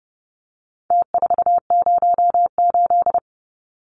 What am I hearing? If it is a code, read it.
Morse code, heard as a keyed tone, about 30 wpm, 700 Hz. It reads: T408
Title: Numbers - Contest Style - 4-Digit 30wpm